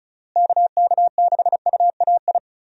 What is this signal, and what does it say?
Morse code, heard as a keyed tone, about 36 wpm, 700 Hz. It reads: KK6UAI
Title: USA Call Signs - 6 Characters - Repeated 3x - 35wpm